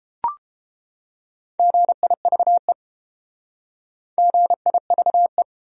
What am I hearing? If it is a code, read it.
Morse code, heard as a keyed tone, about 30 wpm, 700 Hz. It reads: GIVE GIVE